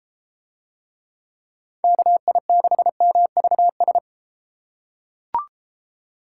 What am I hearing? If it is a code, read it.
Morse code, heard as a keyed tone, about 31 wpm, 700 Hz. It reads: KI6MVS